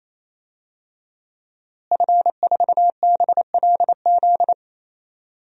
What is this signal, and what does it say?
Morse code, heard as a keyed tone, about 28 wpm, 700 Hz. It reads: F4BLZ
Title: DXCC-100 Call Signs - 5 Characters - Repeated 3x - 28wpm